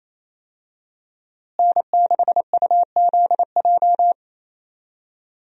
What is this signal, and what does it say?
Morse code, heard as a keyed tone, about 27 wpm, 700 Hz. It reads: N6UZJ
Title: USA Call Signs - 5 Characters - Repeated 3x - 28wpm